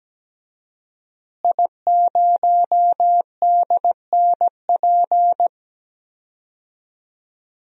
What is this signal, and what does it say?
Morse code, heard as a keyed tone, about 17 wpm, 700 Hz. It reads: I0DNP
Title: DXCC-100 Call Signs - 5 Characters - Repeated 3x - 17wpm